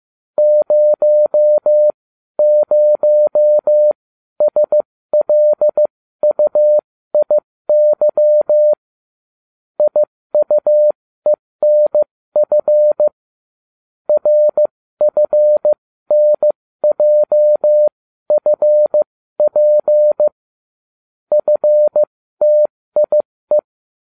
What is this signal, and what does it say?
Morse code, heard as a keyed tone, about 15 wpm, spaced as for 13 wpm, 600 Hz. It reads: 00SLUIY IUENF RFNJFP FTIE